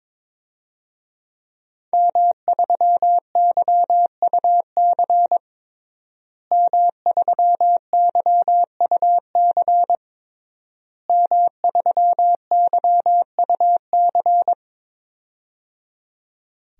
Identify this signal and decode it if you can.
Morse code, heard as a keyed tone, about 22 wpm, 700 Hz. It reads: M3YUC M3YUC M3YUC
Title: DXCC-100 Call Signs - 5 Characters - Repeated 3x - 22wpm